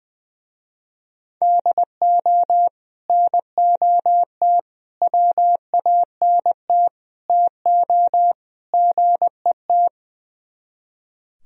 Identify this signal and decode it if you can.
Morse code, heard as a keyed tone, about 21 wpm, 700 Hz. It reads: DO NOT WANT TO GET
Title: Speed-Racing - Sets of 5 Words 20wpm